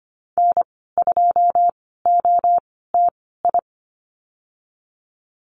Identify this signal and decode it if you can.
Morse code, heard as a keyed tone, about 25 wpm, spaced as for 16 wpm, 700 Hz. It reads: N2OTI
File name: Lesson 15g - Number 5 - Call Signs 25wpm